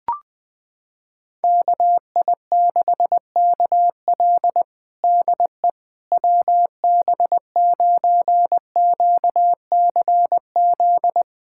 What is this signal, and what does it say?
Morse code, heard as a keyed tone, about 20 wpm, 690 Hz. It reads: KI6KL DE WB9QCZ
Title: USA Callsign DE Callsign 20wpm